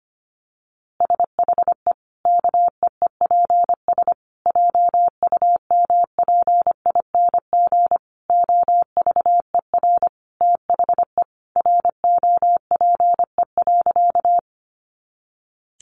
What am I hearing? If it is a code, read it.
Morse code, heard as a keyed tone, about 25 wpm, 700 Hz. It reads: SHE KEEPS JUMPING OVER THE ROPE.
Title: Easy Phrases 25wpm